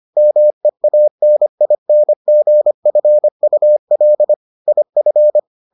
Morse code, heard as a keyed tone, about 25 wpm, 600 Hz. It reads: MEANINGFUL IF